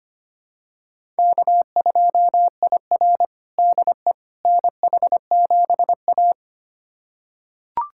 Morse code, heard as a keyed tone, about 25 wpm, 700 Hz. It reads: K2IR DE NH7A
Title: USA Callsign DE Callsign - 4 Characters - 25wpm